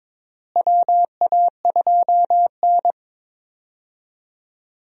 Morse code, heard as a keyed tone, about 22 wpm, 700 Hz. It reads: WA2N
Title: USA Call Signs - 4 Characters - Repeated 2x - 22wpm